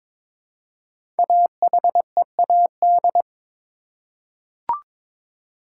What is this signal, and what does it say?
Morse code, heard as a keyed tone, about 22 wpm, 700 Hz. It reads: AHEAD
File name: 5 Letter Words - First 3 Letters Given - 22wpm